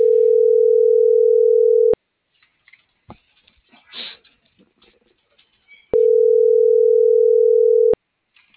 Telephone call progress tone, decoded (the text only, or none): ringback tone